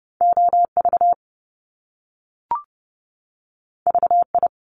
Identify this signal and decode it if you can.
Morse code, heard as a keyed tone, about 30 wpm, 700 Hz. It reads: O V V I
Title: Top Two Letter Combinations in Words 30wpm